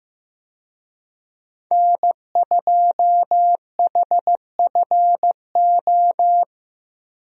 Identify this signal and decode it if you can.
Morse code, heard as a keyed tone, about 15 wpm, 700 Hz. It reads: N2HFO
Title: USA Call Signs - 5 Characters - Repeated 2x - 15wpm